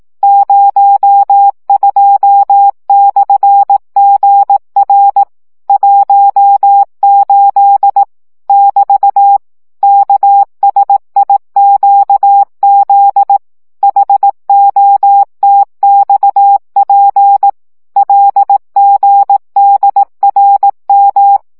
Morse code, heard as a keyed tone, about 18 wpm, 800 Hz. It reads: 02/GR 18 <BT> KSIQZ HOTXP LGDRM
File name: morze vizsga